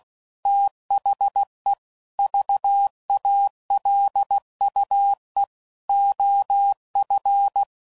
Morse code, heard as a keyed tone, about 16 wpm, 800 Hz. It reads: THE VALUE OF
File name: CW Code Groups @15WPM / Technical Subjects / XC